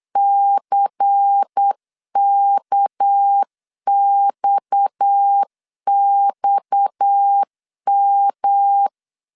Morse code, heard as a keyed tone, about 8 wpm, 800 Hz. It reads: CKXXM
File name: LEZIONI / 6 PKX0 / 6 PKX 8wpm 3041